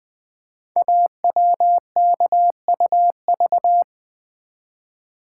Morse code, heard as a keyed tone, about 20 wpm, 700 Hz. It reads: AWKUV
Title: Code Groups - 5 Letters - 20wpm